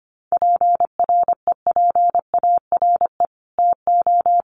Morse code, heard as a keyed tone, about 25 wpm, 700 Hz. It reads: PREPARE TO
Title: Sentences from Top 900 Words 25wpm